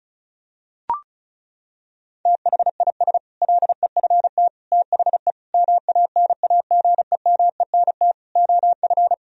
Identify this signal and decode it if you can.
Morse code, heard as a keyed tone, about 34 wpm, 700 Hz. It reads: THIS LEFT THE MANAGEMENT OF
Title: Sentences from Top 1000 Words - Review Part 4 - 35wpm